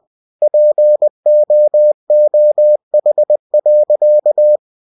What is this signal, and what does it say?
Morse code, heard as a keyed tone, about 20 wpm, 600 Hz. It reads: POOH.